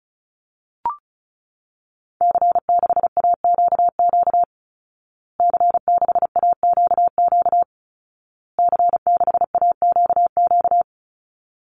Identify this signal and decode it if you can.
Morse code, heard as a keyed tone, about 34 wpm, 700 Hz. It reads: C6AQQ C6AQQ C6AQQ
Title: Call Signs Repeated 2x - 35wpm